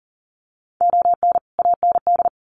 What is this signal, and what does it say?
Morse code, heard as a keyed tone, about 40 wpm, 700 Hz. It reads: ON AND
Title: Sets of 5 Words Encore2 40wpm